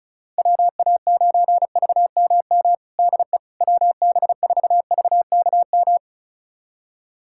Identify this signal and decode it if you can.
Morse code, heard as a keyed tone, about 36 wpm, 690 Hz. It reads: WA9VMM DE WB4VKM
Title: USA Callsign DE Callsign - 6 Characters - 35wpm